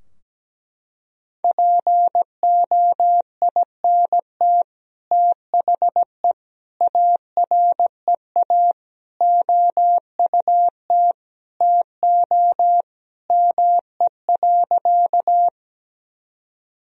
Morse code, heard as a keyed tone, about 17 wpm, 700 Hz. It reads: POINT THE AREA OUT TO ME.
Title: Sentences from Top 1000 Words - Review Part 2 - 17wpm